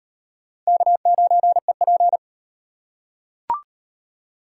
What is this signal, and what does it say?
Morse code, heard as a keyed tone, about 38 wpm, 700 Hz. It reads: K9EP